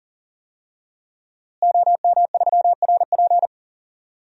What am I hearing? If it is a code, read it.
Morse code, heard as a keyed tone, about 40 wpm, 700 Hz. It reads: OM3RP
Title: DXCC-100 Call Signs - 5 Characters - Repeated 3x - 40wpm